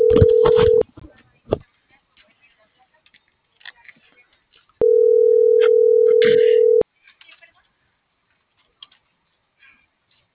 Ringback tone (North America), 440 + 480 Hz.